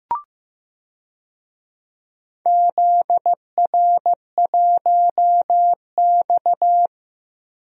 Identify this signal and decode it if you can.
Morse code, heard as a keyed tone, about 15 wpm, 700 Hz. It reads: ZR1X
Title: DXCC-100 Call Signs - 4 Characters - Repeated 3x - 15wpm